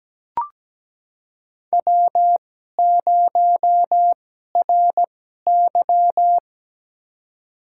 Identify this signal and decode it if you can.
Morse code, heard as a keyed tone, about 17 wpm, spaced as for 12 wpm, 700 Hz. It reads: W0RY
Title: USA Call Signs - 4 Characters - 2x Character Spacing - 17wpm